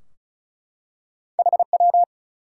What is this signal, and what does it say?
Morse code, heard as a keyed tone, about 36 wpm, 700 Hz. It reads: HW
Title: Top 180 QSO Elements 35wpm